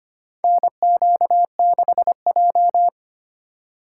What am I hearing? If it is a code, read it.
Morse code, heard as a keyed tone, about 25 wpm, 700 Hz. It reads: NQ6J
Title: USA Call Signs - 4 Characters - 3x Character Spacing - 25wpm